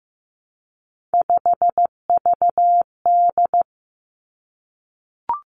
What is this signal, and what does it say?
Morse code, heard as a keyed tone, about 15 wpm, 700 Hz. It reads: 5VD
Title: Three Challenging Letter-Number Combinations 15wpm